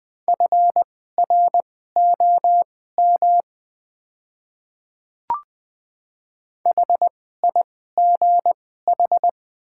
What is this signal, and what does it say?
Morse code, heard as a keyed tone, about 20 wpm, spaced as for 14 wpm, 700 Hz. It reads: FROM HIGH
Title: 2x Character Spacing - Top 200 Words - 20wpm